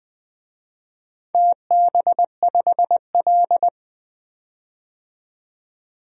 Morse code, heard as a keyed tone, about 20 wpm, 700 Hz. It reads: TB5L
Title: DXCC-100 Call Signs - 4 Characters - 20wpm